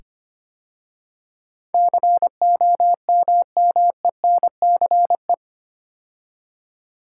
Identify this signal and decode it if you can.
Morse code, heard as a keyed tone, about 25 wpm, 700 Hz. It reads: COMMENCE
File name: 8 Letter Words - First 3 Letters Given - 25wpm